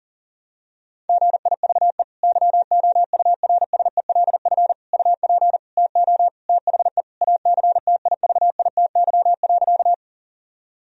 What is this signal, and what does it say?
Morse code, heard as a keyed tone, about 40 wpm, 700 Hz. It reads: GIVE YOURSELF UP TO THE ACTIVITY.
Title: Sentences from Top 1000 Words - Review Part 4 - 40wpm